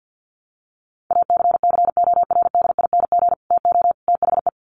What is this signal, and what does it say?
Morse code, heard as a keyed tone, about 50 wpm, 700 Hz. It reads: ACCORDING TO THE
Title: Sets of 3 Words 50wpm